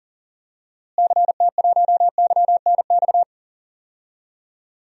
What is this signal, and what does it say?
Morse code, heard as a keyed tone, about 40 wpm, 700 Hz. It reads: CT1YNX
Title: DXCC-100 Call Signs - 6 Characters - Repeated 3x - 40wpm